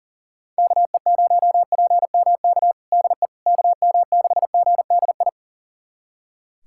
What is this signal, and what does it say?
Morse code, heard as a keyed tone, about 40 wpm, 700 Hz. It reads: KE0PMK DE KM6GDI
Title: USA Callsign DE Callsign - 6 Characters - 40wpm